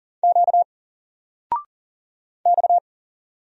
Morse code, heard as a keyed tone, about 39 wpm, 700 Hz. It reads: Q X
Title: Single Letter-Number 40wpm